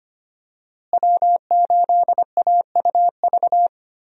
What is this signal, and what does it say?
Morse code, heard as a keyed tone, about 25 wpm, 700 Hz. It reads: W8AUV
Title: USA Call Signs - 5 Characters - Repeated 2x - 25wpm